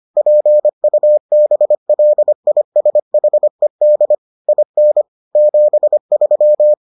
Morse code, heard as a keyed tone, about 25 wpm, 600 Hz. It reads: PUBLISHED IN 73